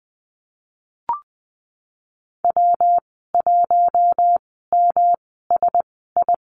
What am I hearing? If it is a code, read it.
Morse code, heard as a keyed tone, about 20 wpm, spaced as for 14 wpm, 700 Hz. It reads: W1MSI
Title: 122 Lesson 20g - Number 3 - Call Signs 20wpm